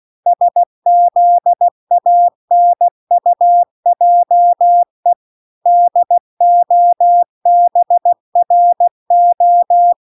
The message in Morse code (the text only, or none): SZANUJE DOBRO